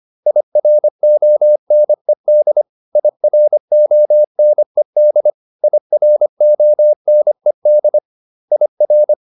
Morse code, heard as a keyed tone, about 25 wpm, 600 Hz. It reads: IRONED IRONED IRONED IR